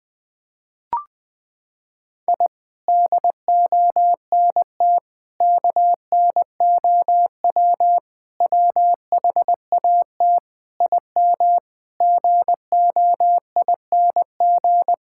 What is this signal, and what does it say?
Morse code, heard as a keyed tone, about 20 wpm, 700 Hz. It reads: I DONT KNOW WHAT IM GOING
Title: Sets of 8 Words - 20wpm